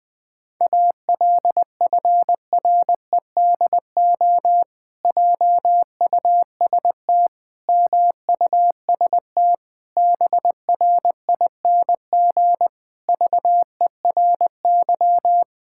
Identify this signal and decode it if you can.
Morse code, heard as a keyed tone, about 20 wpm, 700 Hz. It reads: ALFREDO JUST MUST BRING VERY
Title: Pangram Challenge v2 - 20wpm